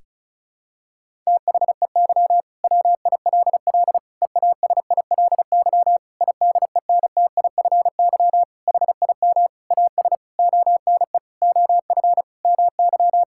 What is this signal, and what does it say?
Morse code, heard as a keyed tone, about 36 wpm, 700 Hz. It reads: THEY WILL EASILY IDENTIFY HIM AS ONE OF MY
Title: Sentences from Top 1000 Words - Review Part 6 - 35wpm